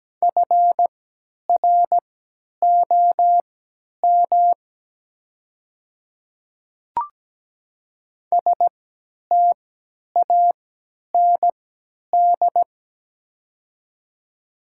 Morse code, heard as a keyed tone, about 17 wpm, spaced as for 10 wpm, 700 Hz. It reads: FROM STAND
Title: 3x Character Spacing - Top 300 Words - 17wpm